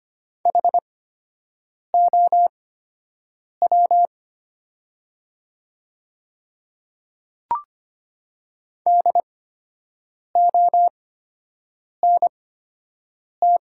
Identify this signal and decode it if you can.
Morse code, heard as a keyed tone, about 25 wpm, spaced as for 7 wpm, 700 Hz. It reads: HOW DONT
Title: 8x Character Spacing - Top 300 Words - 25wpm